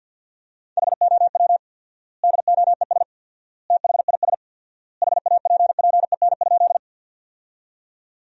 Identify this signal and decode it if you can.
Morse code, heard as a keyed tone, about 50 wpm, 700 Hz. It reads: HOW DOES THIS HAPPEN?
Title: 4x Word-Spacing - Sentences from Top 400 Words - 50wpm